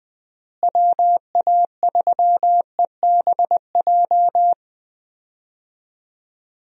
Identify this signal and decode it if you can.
Morse code, heard as a keyed tone, about 20 wpm, 700 Hz. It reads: WA3EBJ
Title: USA Call Signs - 6 Characters - Repeated 3x - 20wpm